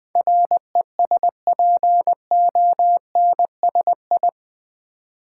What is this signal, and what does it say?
Morse code, heard as a keyed tone, about 20 wpm, 700 Hz. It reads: RESPONSI
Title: Build A Word - 4 Letters At A Time - 11-Letter Words - 20wpm